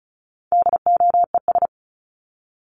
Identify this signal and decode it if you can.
Morse code, heard as a keyed tone, about 34 wpm, 700 Hz. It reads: DOES
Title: ES:S Suffix - Top 100 Words with Base Words 35wpm